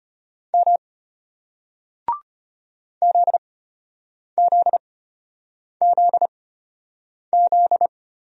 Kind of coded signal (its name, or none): Morse code